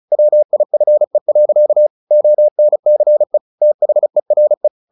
Morse code, heard as a keyed tone, about 35 wpm, 600 Hz. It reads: WIFE. ONCE THERE
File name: the old man and the sea / the old man and the sea 1 35wpm